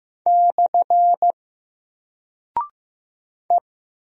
Morse code, heard as a keyed tone, about 15 wpm, 700 Hz. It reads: / E
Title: Single Character (Letter:Number:Punctuation:Prosign) Alt 15wpm